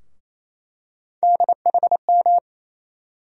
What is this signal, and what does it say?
Morse code, heard as a keyed tone, about 28 wpm, 700 Hz. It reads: DHM
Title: USA Call Sign Postfixes - 3 Letters 28wpm